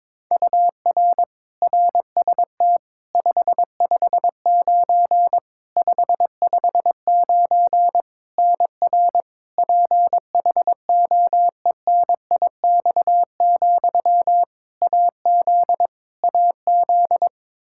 Morse code, heard as a keyed tone, about 22 wpm, 700 Hz. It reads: UR RST 559 559 NR PHOENIX, AZ AZ
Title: Practice QSO - Exchange 1 - Beginning - 22wpm